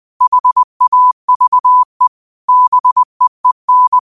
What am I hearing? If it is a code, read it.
Morse code, heard as a keyed tone, about 20 wpm, 1000 Hz. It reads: HAVE BEEN
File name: qotdfiles / 20wpm